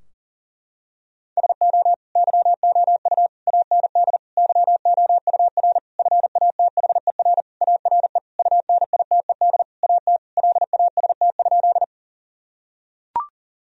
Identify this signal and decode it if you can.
Morse code, heard as a keyed tone, about 40 wpm, 700 Hz. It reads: SO YOU AND YOUR FATHER ARE UNITED AT LAST?
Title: Sentences from Top 1000 Words - Review Part 4 - 40wpm